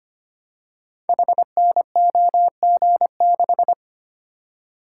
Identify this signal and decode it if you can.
Morse code, heard as a keyed tone, about 25 wpm, 700 Hz. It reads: HNOG6
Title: Code Groups - 5 Letter:Numbers - 25wpm